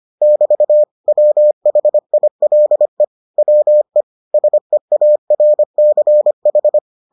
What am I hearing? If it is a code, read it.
Morse code, heard as a keyed tone, about 25 wpm, 600 Hz. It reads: <BT> WHILE WE SEARCH